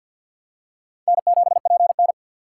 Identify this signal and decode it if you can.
Morse code, heard as a keyed tone, about 49 wpm, 700 Hz. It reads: N7PN
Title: USA Call Signs - 4 Characters - 3x Character Spacing - 50wpm